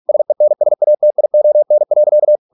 Morse code, heard as a keyed tone, about 46 wpm, 600 Hz. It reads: SENSATION.